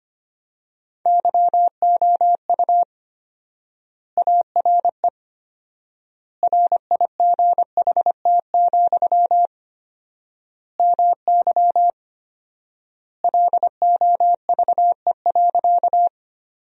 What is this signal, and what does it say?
Morse code, heard as a keyed tone, about 25 wpm, 700 Hz. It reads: YOU ARE RIGHT, MY LOVE.
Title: 4x Word-Spacing - Sentences from Top 400 Words - 25wpm